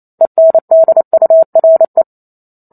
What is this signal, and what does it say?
Morse code, heard as a keyed tone, about 28 wpm, 650 Hz. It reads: ENDURE